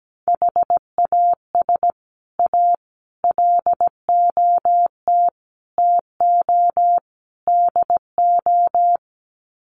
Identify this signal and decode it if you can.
Morse code, heard as a keyed tone, about 17 wpm, 700 Hz. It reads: HAS A LOT TO DO
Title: Sets of 5 Words Encore2 17wpm